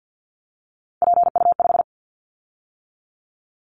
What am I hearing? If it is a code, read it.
Morse code, heard as a keyed tone, about 50 wpm, 700 Hz. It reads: PU5
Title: Callsign Prefixes for Top 50 DXCC - 50wpm